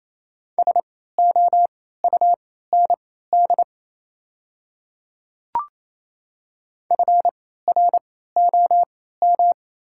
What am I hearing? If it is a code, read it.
Morse code, heard as a keyed tone, about 27 wpm, spaced as for 16 wpm, 690 Hz. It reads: SOUND FROM
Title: 3x Character Spacing - Top 200 Words - 28wpm